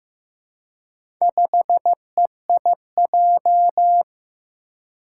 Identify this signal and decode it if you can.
Morse code, heard as a keyed tone, about 15 wpm, 700 Hz. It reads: 5EIJ
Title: Four Challenging Letter-Number Combinations 15wpm